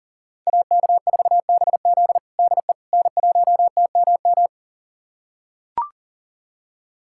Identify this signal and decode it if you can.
Morse code, heard as a keyed tone, about 37 wpm, 700 Hz. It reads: AK4BZ DE N1TMM
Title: USA Callsign DE Callsign - 5 Characters - 40wpm